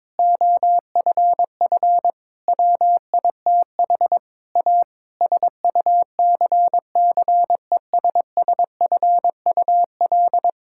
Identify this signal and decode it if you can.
Morse code, heard as a keyed tone, about 22 wpm, 700 Hz. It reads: OFF WITH A SUCCESSFUL
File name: Difficult Sentences with Hints - 22wpm